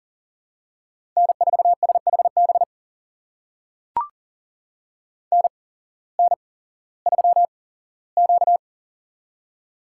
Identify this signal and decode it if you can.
Morse code, heard as a keyed tone, about 42 wpm, spaced as for 19 wpm, 700 Hz. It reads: N4SHB NN3Q
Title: USA Call Signs - 8x Character Spacing - 40wpm